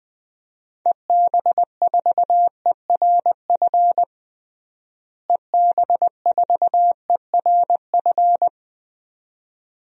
Morse code, heard as a keyed tone, about 20 wpm, 700 Hz. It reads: EB4ERF EB4ERF
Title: DXCC-100 Call Signs - 6 Characters - Repeated 2x - 20wpm